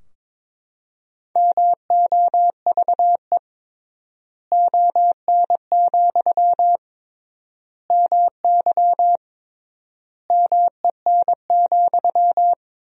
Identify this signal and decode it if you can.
Morse code, heard as a keyed tone, about 22 wpm, 700 Hz. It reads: MOVE ON, MY MEN,
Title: 3x Word-Spacing - Sentences from Top 500 Words - 22wpm